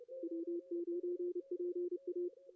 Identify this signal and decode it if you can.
Morse code, heard as a keyed tone, about 31 wpm, 370 Hz, with radio noise. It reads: W9PA
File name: CW / cqww-cw-k5tr